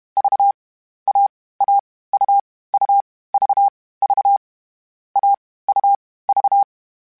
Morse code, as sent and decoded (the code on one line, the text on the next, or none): ...- / .- / .- / ..- / ..- / ...- / ...- / .- / ..- / ...-
V A A U U V V A U V